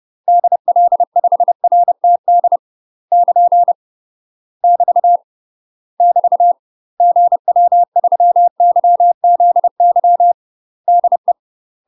Morse code, heard as a keyed tone, about 30 wpm, 700 Hz. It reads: DL5RTD <KN> <BT> <BT> GW3YZY DE
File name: UK MASTER QSOs TRAINING / 30WPM / UK MASTER QSOs 0047